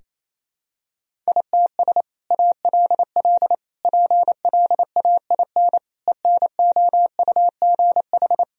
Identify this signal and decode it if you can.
Morse code, heard as a keyed tone, about 28 wpm, 700 Hz. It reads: ITS ALL PLAIN ENOUGH
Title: Sentences from Top 1000 Words - Review Part 2 - 28wpm